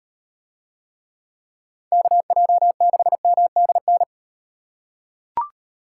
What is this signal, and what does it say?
Morse code, heard as a keyed tone, about 38 wpm, 700 Hz. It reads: KJ6MDN